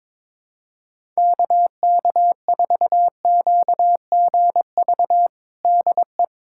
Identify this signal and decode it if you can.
Morse code, heard as a keyed tone, about 22 wpm, 700 Hz. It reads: KK4QGV DE
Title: USA Callsign DE Callsign 22wpm